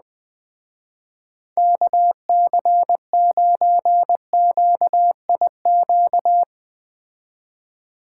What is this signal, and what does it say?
Morse code, heard as a keyed tone, about 20 wpm, 700 Hz. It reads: KC9QIQ